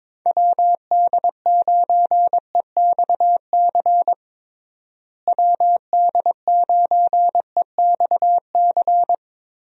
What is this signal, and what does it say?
Morse code, heard as a keyed tone, about 22 wpm, 700 Hz. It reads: WD9EXC WD9EXC
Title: USA Call Signs - 6 Characters - Repeated 3x - 22wpm